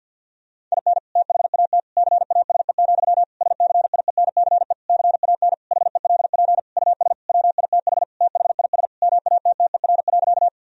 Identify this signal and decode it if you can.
Morse code, heard as a keyed tone, about 48 wpm, 700 Hz. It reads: IN THAT CASE, SCIENCE CAN HELP US WITH THIS MATTER.
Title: Sentences from Top 1000 Words - Review Part 2 - 50wpm